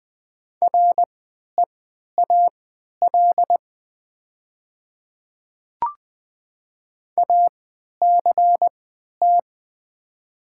Morse code, heard as a keyed tone, about 20 wpm, spaced as for 11 wpm, 700 Hz. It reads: REAL ACT
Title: 3x Character Spacing - Top 300 Words - 20wpm